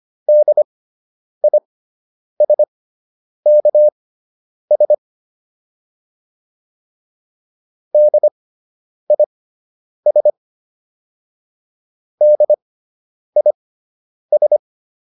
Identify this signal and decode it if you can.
Morse code, heard as a keyed tone, about 25 wpm, spaced as for 9 wpm, 600 Hz. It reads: DISKS DIS DIS